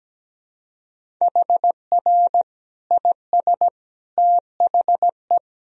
Morse code, heard as a keyed tone, about 17 wpm, 700 Hz. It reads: HR IS THE
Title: Sentences from Top 200 Words and Abbreviations 17wpm